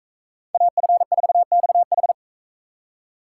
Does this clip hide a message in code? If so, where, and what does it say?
Yes: Morse code from 0.5 to 2.1 s, heard as a keyed tone, about 41 wpm, 700 Hz. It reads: AF4XH